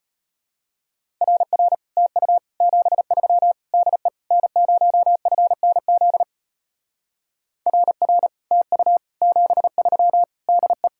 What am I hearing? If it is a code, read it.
Morse code, heard as a keyed tone, about 33 wpm, 700 Hz. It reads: RR TU 73 DE N0FNZ RR TU 73 DE